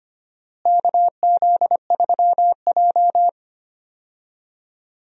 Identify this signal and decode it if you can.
Morse code, heard as a keyed tone, about 25 wpm, 700 Hz. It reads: KZ3J
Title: USA Call Signs - 4 Characters - Repeated 3x - 25wpm